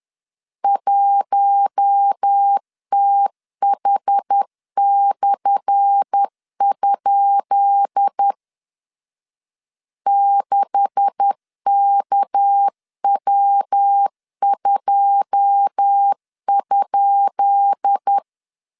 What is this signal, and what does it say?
Morse code, heard as a keyed tone, about 10 wpm, 800 Hz. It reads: 1TH/? 6KW2?
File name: LEZIONI / 10 SIGNS / 10B INT 10wpm 2844